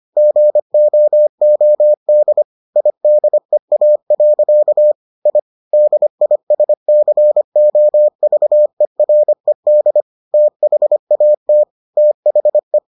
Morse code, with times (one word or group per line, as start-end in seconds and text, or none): 0.2-2.4 GOOD
2.8-4.9 IDEA.
5.3-5.4 I
5.7-10.0 DISCOVERED
10.3-11.6 THAT
12.0-12.8 THE